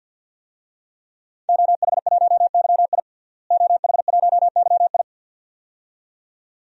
Morse code, heard as a keyed tone, about 50 wpm, 700 Hz. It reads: OH1YI OH1YI
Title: DXCC-100 Call Signs - 5 Characters - Repeated 2x - 50wpm